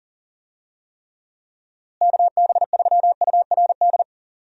Morse code, heard as a keyed tone, about 40 wpm, 700 Hz. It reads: KB3URD
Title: USA Call Signs - 6 Characters - Repeated 3x - 40wpm